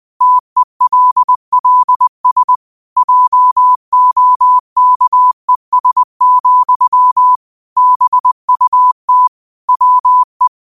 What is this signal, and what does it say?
Morse code, heard as a keyed tone, about 20 wpm, 1000 Hz. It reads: TELLS JOKES, BUT WE
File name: qotdfiles / 20wpm